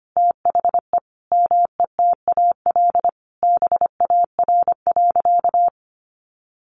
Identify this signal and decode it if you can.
Morse code, heard as a keyed tone, about 25 wpm, 700 Hz. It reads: THE METAL BAR.
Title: Additional Easy Phrases 25wpm